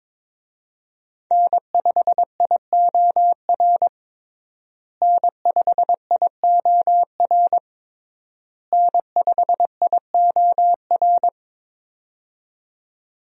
Morse code, heard as a keyed tone, about 22 wpm, 700 Hz. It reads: N5IOR N5IOR N5IOR
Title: USA Call Signs - 5 Characters - Repeated 3x - 22wpm